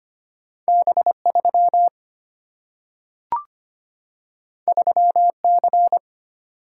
Morse code, heard as a keyed tone, about 25 wpm, 700 Hz. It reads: B 3 3 C
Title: Single-Letter Single-Number 25wpm